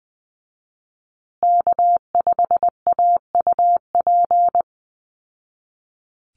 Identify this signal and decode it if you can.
Morse code, heard as a keyed tone, about 20 wpm, 700 Hz. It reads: K5AUP
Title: Call Signs Repeated 2x - 20wpm